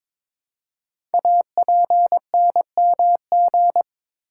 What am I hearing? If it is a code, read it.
Morse code, heard as a keyed tone, about 22 wpm, 700 Hz. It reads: APNMG
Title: Code Groups - 5 Letters - 22wpm